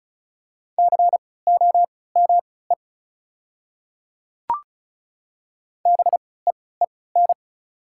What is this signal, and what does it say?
Morse code, heard as a keyed tone, about 36 wpm, spaced as for 20 wpm, 690 Hz. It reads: COME BEEN
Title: 3x Character Spacing - Top 100 Words - 35wpm